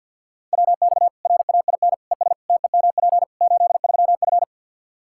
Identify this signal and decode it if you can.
Morse code, heard as a keyed tone, about 49 wpm, 700 Hz. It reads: WX RAIN ES TEMP 83F
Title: Practice QSO - Exchange 2 - Info - 50wpm